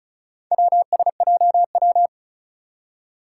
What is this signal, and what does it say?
Morse code, heard as a keyed tone, about 36 wpm, 700 Hz. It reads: WSJW
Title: Four Challenging Letter-Number Combinations 35wpm